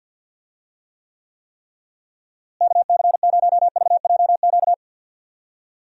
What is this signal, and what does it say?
Morse code, heard as a keyed tone, about 50 wpm, 700 Hz. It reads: KC0VJQ
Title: USA Call Signs - 6 Characters - Repeated 3x - 50wpm